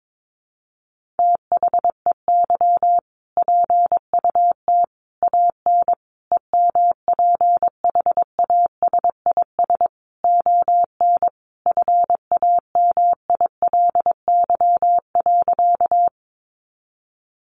Morse code, heard as a keyed tone, about 22 wpm, 700 Hz. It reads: THEY PUT AN EMPHASIS ON FAMILY.
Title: Sentences from Top 1400 Words 22wpm